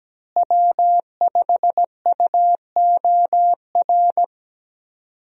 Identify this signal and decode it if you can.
Morse code, heard as a keyed tone, about 17 wpm, 700 Hz. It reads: W5UOR
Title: USA Call Signs - 5 Characters - Repeated 3x - 17wpm